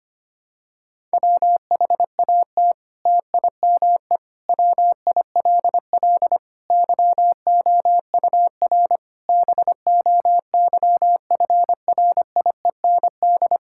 Morse code, heard as a keyed tone, about 25 wpm, 700 Hz. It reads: WHAT TIME WILL YOUR BOYFRIEND
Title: Sentences from Top 1000 Words - Review Part 5 - 25wpm